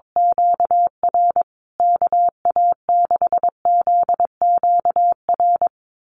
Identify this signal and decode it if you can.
Morse code, heard as a keyed tone, about 22 wpm, 700 Hz. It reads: QR KA6ZQR
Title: USA Call Signs Repeated 3x 22wpm